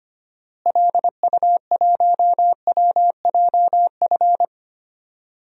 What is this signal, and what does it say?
Morse code, heard as a keyed tone, about 25 wpm, 700 Hz. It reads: LU1WJF
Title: DXCC-100 Call Signs - 6 Characters - Repeated 2x - 25wpm